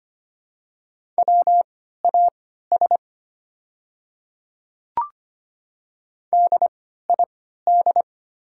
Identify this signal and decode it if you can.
Morse code, heard as a keyed tone, about 25 wpm, spaced as for 14 wpm, 690 Hz. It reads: WAS DID
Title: 3x Character Spacing - Top 100 Words - 25wpm